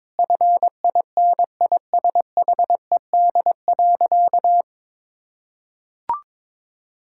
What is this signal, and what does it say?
Morse code, heard as a keyed tone, about 21 wpm, 700 Hz. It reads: FINISHED.
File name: practice / 22wpm / Sentences with 3-Word Phrases - 22wpm